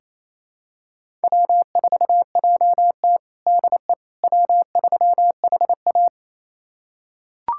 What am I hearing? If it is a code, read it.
Morse code, heard as a keyed tone, about 28 wpm, 700 Hz. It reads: W4JT DE W3HA
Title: USA Callsign DE Callsign - 4 Characters - 28wpm